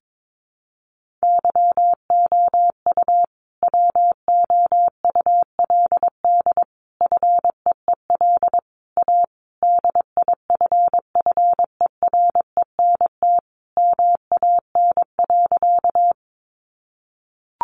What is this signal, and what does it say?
Morse code, heard as a keyed tone, about 22 wpm, 700 Hz. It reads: YOU WOULD FEEL A DIFFERENT MAN.
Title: Sentences from Top 600 Words 22wpm